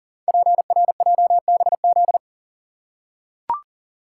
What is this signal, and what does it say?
Morse code, heard as a keyed tone, about 39 wpm, 690 Hz. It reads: PRJBZ
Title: Code Groups - 5 Letters - 40wpm